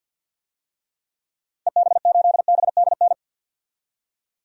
Morse code, heard as a keyed tone, about 47 wpm, 700 Hz. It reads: EB8BDN
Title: DXCC-100 Call Signs - 6 Characters - 50wpm